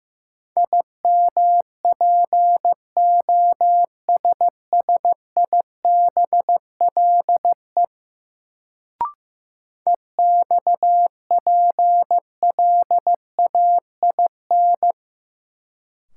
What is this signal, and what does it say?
Morse code, heard as a keyed tone, about 15 wpm, 700 Hz. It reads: IMPOSSIBLE EXPLAIN
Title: Sentences from Top 1100 Words - New Word Drill Alt - 15wpm